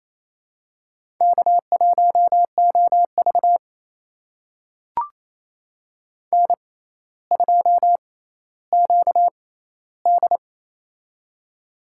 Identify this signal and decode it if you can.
Morse code, heard as a keyed tone, about 28 wpm, spaced as for 14 wpm, 700 Hz. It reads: K1OV N2QD
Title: USA Call Signs - 4 Characters - 6x Character Spacing - 28wpm